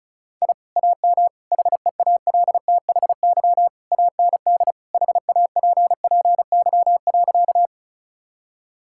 Morse code, heard as a keyed tone, about 36 wpm, 700 Hz. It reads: I AM HEALTHY AND HAPPY.
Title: Sentences from Top 1000 Words - Review Part 5 - 35wpm